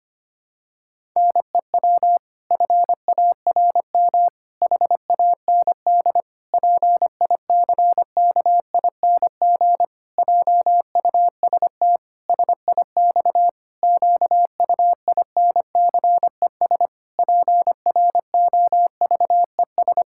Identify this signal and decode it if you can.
Morse code, heard as a keyed tone, about 25 wpm, 700 Hz. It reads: NEW FARM HAND PICKING JUST SIX QUINCES PROVES
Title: Pangram Challenge v2 - 25wpm